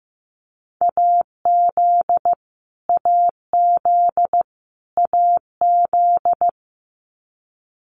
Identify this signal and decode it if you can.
Morse code, heard as a keyed tone, about 15 wpm, 700 Hz. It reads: AZ AZ AZ
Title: US States Repeated 3x 15wpm